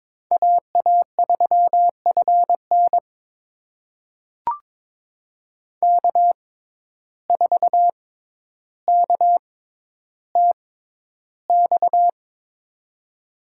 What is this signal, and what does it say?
Morse code, heard as a keyed tone, about 22 wpm, spaced as for 11 wpm, 700 Hz. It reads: AA3FN K4KTX
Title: USA Call Signs - 5 Characters - 6x Character Spacing - 22wpm